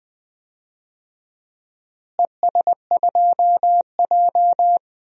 Morse code, heard as a keyed tone, about 20 wpm, 700 Hz. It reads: ES2J
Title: DXCC-100 Call Signs - 4 Characters - Repeated 2x - 20wpm